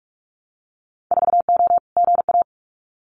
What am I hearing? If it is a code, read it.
Morse code, heard as a keyed tone, about 46 wpm, 700 Hz. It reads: 4O GA
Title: Winter Field Day Exchanges 45wpm